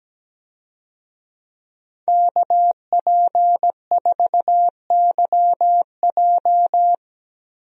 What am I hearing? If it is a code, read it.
Morse code, heard as a keyed tone, about 17 wpm, 700 Hz. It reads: KP4YJ
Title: DXCC-100 Call Signs - 5 Characters - Repeated 2x - 17wpm